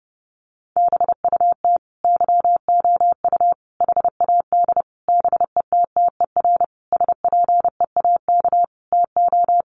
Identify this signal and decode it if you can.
Morse code, heard as a keyed tone, about 30 wpm, 700 Hz. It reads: BUT YOU HAD BETTER SPEAK TO
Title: Sentences from Top 1100 Words 30wpm